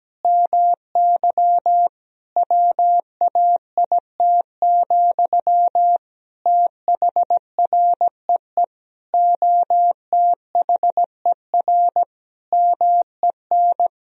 Morse code, heard as a keyed tone, about 17 wpm, 700 Hz. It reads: MY WAIT, THREE OTHER MEN
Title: Sentences from Top 1000 Words - Review Part 2 - 17wpm